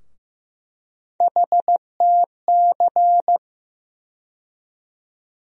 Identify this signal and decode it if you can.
Morse code, heard as a keyed tone, about 15 wpm, 700 Hz. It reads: HTC
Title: USA Call Sign Postfixes - 3 Letters 15wpm